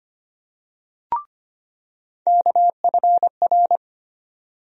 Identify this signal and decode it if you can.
Morse code, heard as a keyed tone, about 25 wpm, 700 Hz. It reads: KFR
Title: USA Call Sign Postfixes - 3 Letters 25wpm